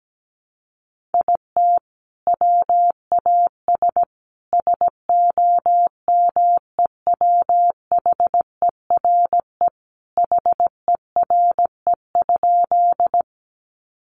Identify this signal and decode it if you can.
Morse code, heard as a keyed tone, about 17 wpm, 700 Hz. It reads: IT WAS SOMEWHERE HERE?
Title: Sentences from Top 1400 Words 17wpm